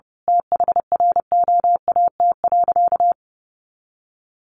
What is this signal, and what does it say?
Morse code, heard as a keyed tone, about 30 wpm, 700 Hz. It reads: THROAT.
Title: A Sound Of Thunder 30wpm